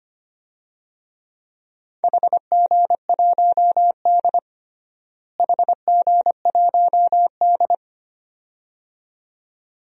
Morse code, heard as a keyed tone, about 25 wpm, 700 Hz. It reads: HG1D HG1D
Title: DXCC-100 Call Signs - 4 Characters - Repeated 2x - 25wpm